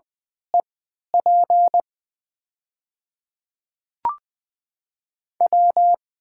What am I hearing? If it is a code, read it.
Morse code, heard as a keyed tone, about 20 wpm, spaced as for 2 wpm, 700 Hz. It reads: E PW
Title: 3x Character Spacing - Top 500 Words - 20wpm